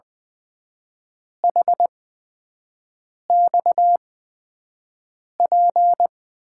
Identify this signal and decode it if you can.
Morse code, heard as a keyed tone, about 20 wpm, spaced as for 5 wpm, 700 Hz. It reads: HXP
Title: USA Call Signs - 8x Character Spacing - 20wpm